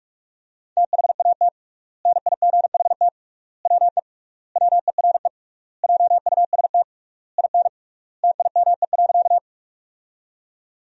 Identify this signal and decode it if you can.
Morse code, heard as a keyed tone, about 46 wpm, 700 Hz. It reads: THAT NIGHT WE WERE JUST IN TIME.
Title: 3x Word-Spacing - Sentences from Top 300 Words - 45wpm